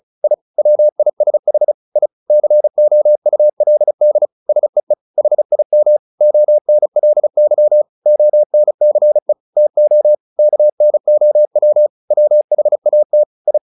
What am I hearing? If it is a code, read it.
Morse code, heard as a keyed tone, about 34 wpm, 600 Hz. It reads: I WISH I COULD SEE HIM ONLY ONCE TO KNOW WHAT I
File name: the old man and the sea / the old man and the sea 1 35wpm